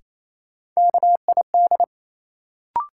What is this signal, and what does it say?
Morse code, heard as a keyed tone, about 28 wpm, 700 Hz. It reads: KID
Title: Sets of 5 Words - New Word Drill - 28wpm